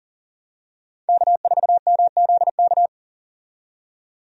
Morse code, heard as a keyed tone, about 40 wpm, 700 Hz. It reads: K4MZK
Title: USA Call Signs - 5 Characters - 40wpm